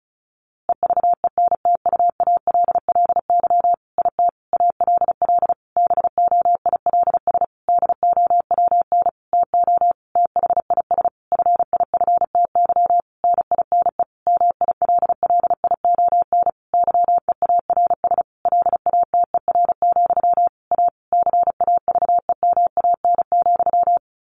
Morse code, heard as a keyed tone, about 34 wpm, 700 Hz. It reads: EVENTUALLY IT ALL BOILS DOWN TO THIS FIFTY NINE MILLION YEARS LATER, A CAVEMAN,
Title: A Sound Of Thunder 35wpm